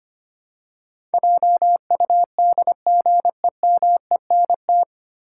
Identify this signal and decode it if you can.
Morse code, heard as a keyed tone, about 25 wpm, 700 Hz. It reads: JUDGEMENT
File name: Nine Letter Words - 25wpm